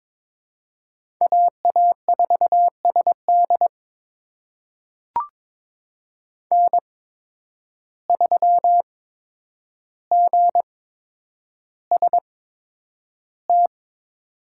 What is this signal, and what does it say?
Morse code, heard as a keyed tone, about 22 wpm, spaced as for 9 wpm, 700 Hz. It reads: AA4SD N3GST
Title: USA Call Signs - 5 Characters - 8x Character Spacing - 22wpm